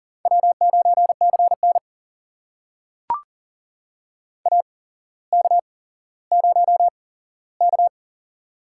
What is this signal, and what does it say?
Morse code, heard as a keyed tone, about 40 wpm, spaced as for 17 wpm, 700 Hz. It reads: W9CN AK0K
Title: USA Call Signs - 4 Characters - 8x Character Spacing - 40wpm